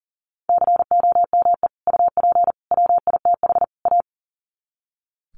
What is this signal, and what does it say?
Morse code, heard as a keyed tone, about 40 wpm, 700 Hz. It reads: COME UP WITH A
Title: Sets of 4 Words 40wpm